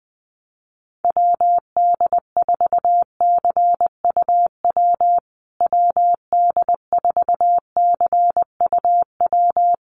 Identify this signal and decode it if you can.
Morse code, heard as a keyed tone, about 20 wpm, 700 Hz. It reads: WD4CUW WD4CUW
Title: USA Call Signs Repeated 3x 20wpm